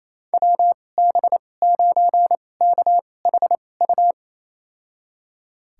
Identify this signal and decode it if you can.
Morse code, heard as a keyed tone, about 28 wpm, spaced as for 20 wpm, 700 Hz. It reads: WB9KHU
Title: USA Call Signs - 2x Character Spacing - 28wpm